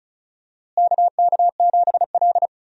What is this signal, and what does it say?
Morse code, heard as a keyed tone, about 36 wpm, 700 Hz. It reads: KK7L
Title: USA Call Signs - 4 Characters - Repeated 3x - 35wpm